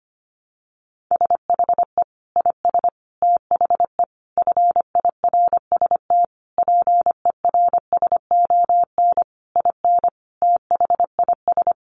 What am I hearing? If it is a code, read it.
Morse code, heard as a keyed tone, about 25 wpm, 700 Hz. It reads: SHE IS THE FIRST PERSON IN THIS
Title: Sentences from Top 400 Words 25wpm